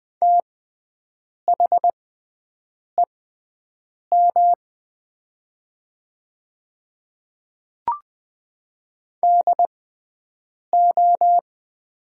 Morse code, heard as a keyed tone, about 20 wpm, spaced as for 7 wpm, 700 Hz. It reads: THEM DO
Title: 6x Character Spacing - Top 100 Words - 20wpm